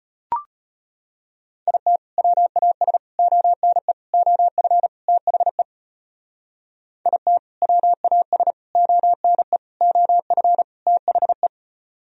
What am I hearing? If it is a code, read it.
Morse code, heard as a keyed tone, about 30 wpm, 700 Hz. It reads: IT WAS ONE OF THE IT WAS ONE OF THE
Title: Speed-Racing - Sets of 5 Words 25wpm